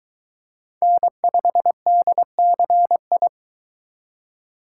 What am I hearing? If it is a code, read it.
Morse code, heard as a keyed tone, about 22 wpm, 700 Hz. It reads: N5DCI